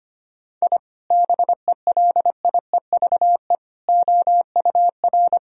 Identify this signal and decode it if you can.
Morse code, heard as a keyed tone, about 25 wpm, 700 Hz. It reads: I BELIEVE OUR
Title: Sentences from Top 1000 Words - Review Part 3 - 25wpm